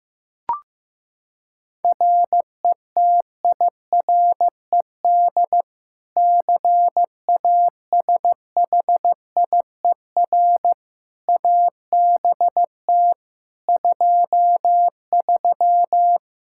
Morse code, heard as a keyed tone, about 16 wpm, 700 Hz. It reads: RETIRED CASHIER ABT 23
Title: Practice QSO - Exchange 3 - More Info - 15wpm